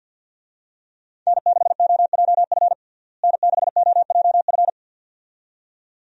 Morse code, heard as a keyed tone, about 47 wpm, 700 Hz. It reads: N6OJF N6OJF
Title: USA Call Signs - 5 Characters - Repeated 2x - 50wpm